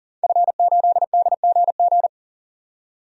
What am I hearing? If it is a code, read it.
Morse code, heard as a keyed tone, about 40 wpm, 700 Hz. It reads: F8DGG
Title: DXCC-100 Call Signs - 5 Characters - 40wpm